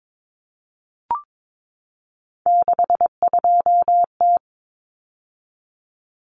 Morse code, heard as a keyed tone, about 22 wpm, 700 Hz. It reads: 62T
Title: Common Cut Numbers - 3-Digit 22wpm